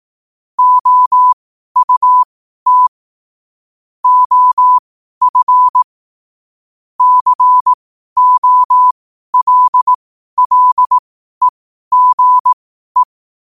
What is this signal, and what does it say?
Morse code, heard as a keyed tone, about 18 wpm, spaced as for 12 wpm, 1000 Hz. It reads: OUT OF COLLEGE
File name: qotdfiles / 12wpm